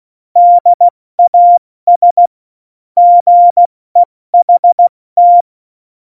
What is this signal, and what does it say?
Morse code, heard as a keyed tone, about 16 wpm, spaced as for 14 wpm, 700 Hz. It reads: DAS GEHT
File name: Morsekurs / Witze / Witz 21. Januar 2020, 70 F80